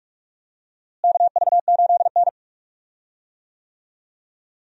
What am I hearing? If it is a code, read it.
Morse code, heard as a keyed tone, about 46 wpm, 700 Hz. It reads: KV8N
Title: USA Call Signs - 4 Characters - Repeated 3x - 45wpm